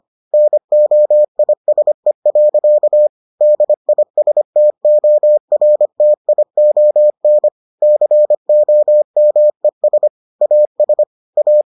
Morse code, heard as a keyed tone, about 25 wpm, 600 Hz. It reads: NOISE. DISTORTION COMES AS A